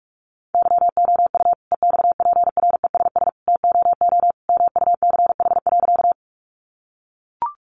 Morse code, heard as a keyed tone, about 44 wpm, 700 Hz. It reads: YOU EXPRESS TOO MUCH.
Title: Sentences from Top 1100 Words 45wpm